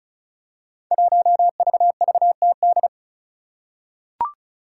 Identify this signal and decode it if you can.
Morse code, heard as a keyed tone, about 36 wpm, 700 Hz. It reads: 1VVTD
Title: Five Challenging Letter-Number Combinations 35wpm